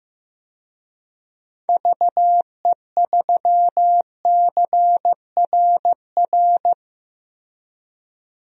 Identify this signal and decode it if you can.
Morse code, heard as a keyed tone, about 15 wpm, 700 Hz. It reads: VE3CRR
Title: DXCC-100 Call Signs - 6 Characters - Repeated 3x - 15wpm